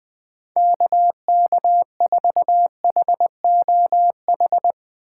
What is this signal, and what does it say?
Morse code, heard as a keyed tone, about 20 wpm, 700 Hz. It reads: KK4HOH
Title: USA Call Signs - 6 Characters - 6x Character Spacing - 20wpm